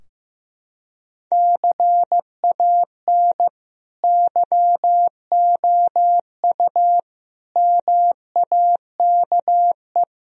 Morse code, heard as a keyed tone, about 16 wpm, 700 Hz. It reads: CAN YOU MAKE
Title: Sentences from Top 1000 Words - Review Part 4 - 15wpm